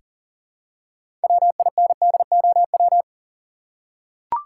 Morse code, heard as a keyed tone, about 38 wpm, 700 Hz. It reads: WINDOW
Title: Six Letter Words - 40wpm